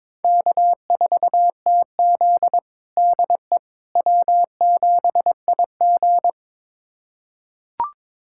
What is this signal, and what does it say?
Morse code, heard as a keyed tone, about 21 wpm, 700 Hz. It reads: K4TZ DE W7IG
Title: USA Callsign DE Callsign - 4 Characters - 22wpm